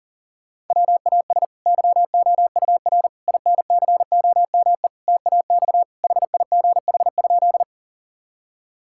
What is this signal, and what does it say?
Morse code, heard as a keyed tone, about 37 wpm, 700 Hz. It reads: WAS YOUR INCOME TAX HIGH?
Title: Sentences from Top 1000 Words - Review Part 5 - 40wpm